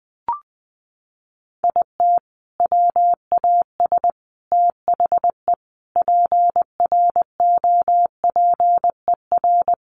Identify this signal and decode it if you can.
Morse code, heard as a keyed tone, about 20 wpm, 700 Hz. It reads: IT WAS THE PROPER
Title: Sentences from Top 1300 Words 20wpm